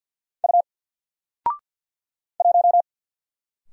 Morse code, heard as a keyed tone, about 49 wpm, 700 Hz. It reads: U 1
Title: Single Letter-Number Alt 50wpm